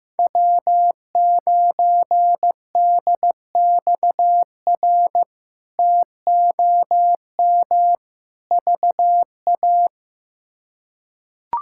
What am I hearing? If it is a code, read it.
Morse code, heard as a keyed tone, about 16 wpm, 700 Hz. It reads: W9DXR TOM VA
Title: K1USN SST QSO Practice - 15wpm